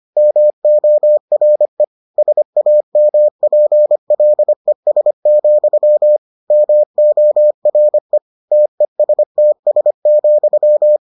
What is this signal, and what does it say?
Morse code, heard as a keyed tone, about 25 wpm, 600 Hz. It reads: MORE SAMPLES, MORE TESTS,